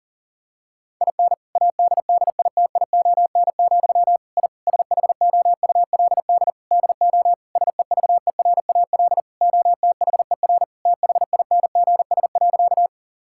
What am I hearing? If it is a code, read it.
Morse code, heard as a keyed tone, about 40 wpm, 700 Hz. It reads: IN ADDITION, I SHOULD DO SEVERAL OTHER THINGS.
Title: Sentences from Top 1000 Words - Review Part 6 - 40wpm